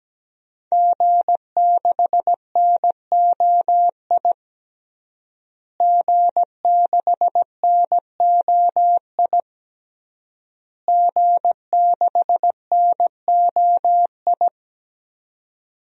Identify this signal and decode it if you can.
Morse code, heard as a keyed tone, about 17 wpm, 700 Hz. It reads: G6NOI G6NOI G6NOI
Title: DXCC-100 Call Signs - 5 Characters - Repeated 3x - 17wpm